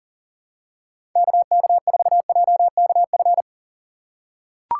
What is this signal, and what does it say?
Morse code, heard as a keyed tone, about 37 wpm, 700 Hz. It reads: KK4JKF
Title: USA Call Signs - 3x Character Spacing - 40wpm